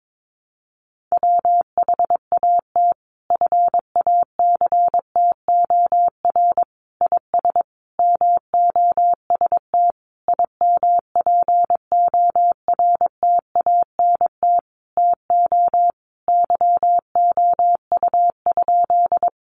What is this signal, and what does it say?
Morse code, heard as a keyed tone, about 22 wpm, 700 Hz. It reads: WHAT FACTOR IS MOST IMPORTANT TO YOU?
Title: Sentences from Top 1300 Words 22wpm